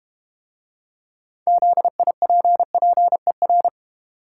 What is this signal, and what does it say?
Morse code, heard as a keyed tone, about 32 wpm, 700 Hz. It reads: ZIPPER